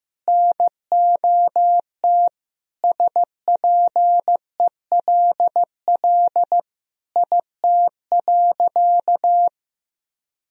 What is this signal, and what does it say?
Morse code, heard as a keyed tone, about 15 wpm, 700 Hz. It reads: NOT SPELL IT.
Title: Sentences from Top 1000 Words - Review Part 1 - 15wpm